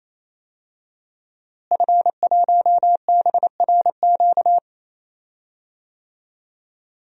Morse code, heard as a keyed tone, about 28 wpm, 700 Hz. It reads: F1BRQ
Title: DXCC-100 Call Signs - 5 Characters - Repeated 3x - 28wpm